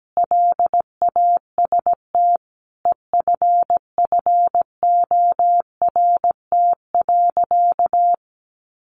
Morse code, heard as a keyed tone, about 17 wpm, 700 Hz. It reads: LAST EFFORT.
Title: Sentences from Top 1200 Words 17wpm